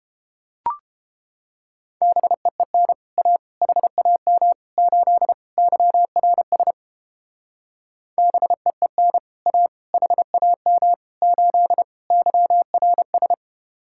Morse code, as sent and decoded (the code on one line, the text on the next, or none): -... . -. / .- / .... .- -- / ---.. / -.-- .-. ... / -... . -. / .- / .... .- -- / ---.. / -.-- .-. ...
BEEN A HAM 8 YRS BEEN A HAM 8 YRS